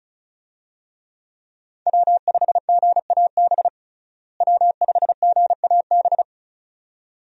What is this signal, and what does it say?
Morse code, heard as a keyed tone, about 36 wpm, 700 Hz. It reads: W5GAB W5GAB
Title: USA Call Signs - 5 Characters - Repeated 2x - 35wpm